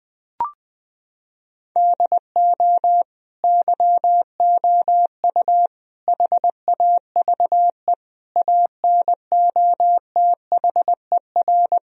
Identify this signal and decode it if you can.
Morse code, heard as a keyed tone, about 20 wpm, 700 Hz. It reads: DO YOU HAVE ANOTHER
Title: Sentences from Top 1000 Words - Review Part 4 - 20wpm